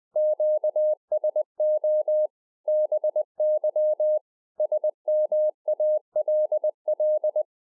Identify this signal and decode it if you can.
Morse code, heard as a keyed tone, about 20 wpm, 610 Hz. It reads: QSO BY SMALL